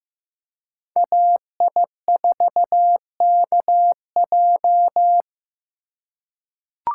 Morse code, heard as a keyed tone, about 15 wpm, 700 Hz. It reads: AI4KJ
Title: USA Call Signs - 2x Character Spacing - 15wpm